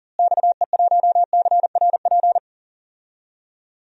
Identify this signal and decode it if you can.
Morse code, heard as a keyed tone, about 37 wpm, 700 Hz. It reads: XE1CRP
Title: DXCC-100 Call Signs - 6 Characters - Repeated 3x - 40wpm